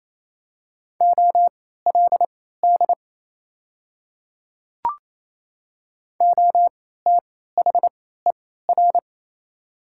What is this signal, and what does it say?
Morse code, heard as a keyed tone, about 28 wpm, spaced as for 16 wpm, 700 Hz. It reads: OLD OTHER
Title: 3x Character Spacing - Top 200 Words - 28wpm